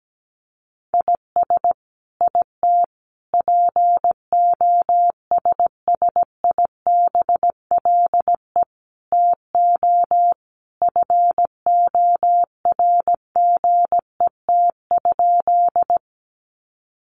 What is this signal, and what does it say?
Morse code, heard as a keyed tone, about 17 wpm, 700 Hz. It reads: IS IT POSSIBLE TO FORGET?
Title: Sentences from Top 1200 Words 17wpm